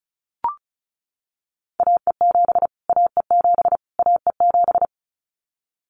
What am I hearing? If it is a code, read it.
Morse code, heard as a keyed tone, about 34 wpm, 700 Hz. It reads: AE7 AE7 AE7
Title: Callsign Prefixes for Top 25 DXCC - Repeated 3x - 35wpm